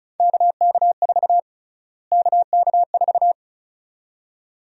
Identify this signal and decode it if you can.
Morse code, heard as a keyed tone, about 34 wpm, 700 Hz. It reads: KK4 KK4
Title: USA Call Sign Prefixes - Repeated 2x - 35wpm